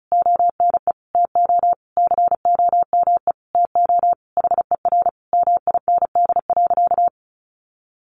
Morse code, heard as a keyed tone, about 34 wpm, 700 Hz. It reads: ONE TO COME TO HER MIND.
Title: Sentences from Top 1300 Words 35wpm